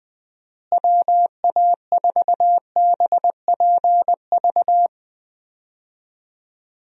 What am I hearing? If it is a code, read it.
Morse code, heard as a keyed tone, about 21 wpm, 700 Hz. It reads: WA4BPV
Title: USA Call Signs - 6 Characters - Repeated 3x - 20wpm